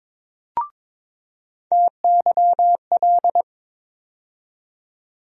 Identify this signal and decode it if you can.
Morse code, heard as a keyed tone, about 22 wpm, 700 Hz. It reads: TYL
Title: USA Call Sign Postfixes - 3 Letters 22wpm